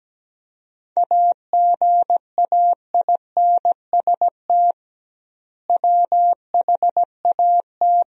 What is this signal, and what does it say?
Morse code, heard as a keyed tone, about 17 wpm, 700 Hz. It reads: AGAINST WHAT
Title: 2x Word-Spacing - Sentences from Top 400 Words - 17wpm